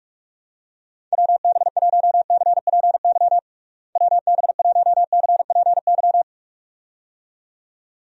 Morse code, heard as a keyed tone, about 41 wpm, 700 Hz. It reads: WB1CPY WB1CPY
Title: USA Call Signs - 6 Characters - Repeated 2x - 45wpm